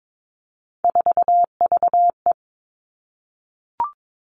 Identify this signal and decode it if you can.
Morse code, heard as a keyed tone, about 22 wpm, 700 Hz. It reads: 4VE
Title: Three Challenging Letter-Number Combinations 22wpm